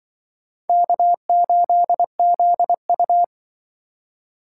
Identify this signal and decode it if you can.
Morse code, heard as a keyed tone, about 25 wpm, 700 Hz. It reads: K8ZU
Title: Speed Racing - USA 1x2 Call Signs 20wpm